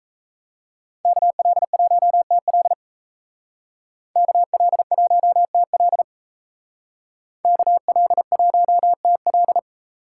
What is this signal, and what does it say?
Morse code, heard as a keyed tone, about 36 wpm, 700 Hz. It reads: KL1TL KL1TL KL1TL